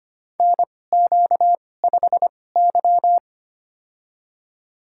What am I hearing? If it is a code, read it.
Morse code, heard as a keyed tone, about 25 wpm, spaced as for 18 wpm, 700 Hz. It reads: NQ5Y
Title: USA Call Signs - 4 Characters - 2x Character Spacing - 25wpm